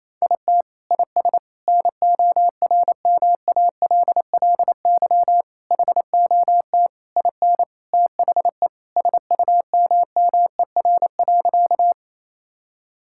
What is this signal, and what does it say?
Morse code, heard as a keyed tone, about 28 wpm, 700 Hz. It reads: IT IS NORMALLY HOT IN THE SUMMER.
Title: Sentences from Top 1000 Words - Review Part 6 - 28wpm